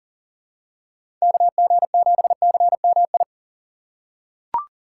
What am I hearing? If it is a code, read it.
Morse code, heard as a keyed tone, about 40 wpm, 700 Hz. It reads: KG7CMI
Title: USA Call Signs - 6 Characters - 4x Character Spacing - 40wpm